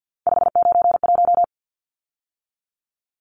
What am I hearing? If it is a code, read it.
Morse code, heard as a keyed tone, about 50 wpm, 700 Hz. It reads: 591
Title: Numbers - 3-Digit 50wpm